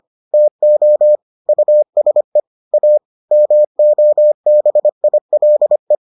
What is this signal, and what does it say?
Morse code, heard as a keyed tone, about 25 wpm, 600 Hz. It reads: TO USE A MOBILE